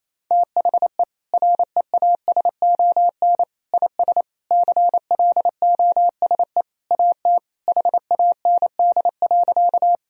Morse code, heard as a keyed tone, about 28 wpm, 700 Hz. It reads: THE REASON IS CLOSE AT HAND.
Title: Sentences from Top 1000 Words - Review Part 1 - 28wpm